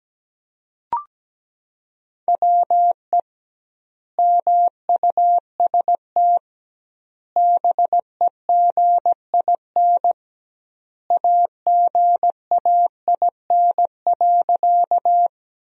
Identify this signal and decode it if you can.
Morse code, heard as a keyed tone, about 17 wpm, 700 Hz. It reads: WE MUST BEGIN AGAIN.
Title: 2x Word-Spacing - Sentences from Top 300 Words - 17wpm